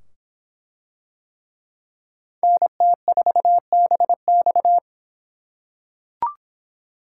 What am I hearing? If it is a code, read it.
Morse code, heard as a keyed tone, about 24 wpm, 700 Hz. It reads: NT4BX